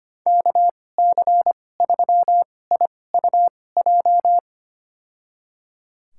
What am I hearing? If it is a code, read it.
Morse code, heard as a keyed tone, about 25 wpm, spaced as for 18 wpm, 700 Hz. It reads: KC3IUJ
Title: USA Call Signs - 2x Character Spacing - 25wpm